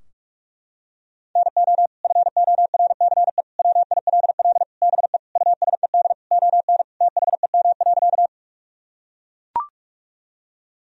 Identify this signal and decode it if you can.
Morse code, heard as a keyed tone, about 41 wpm, 700 Hz. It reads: NO FORCE WILL BE USED ON THEM.
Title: Sentences from Top 1000 Words - Review Part 3 - 45wpm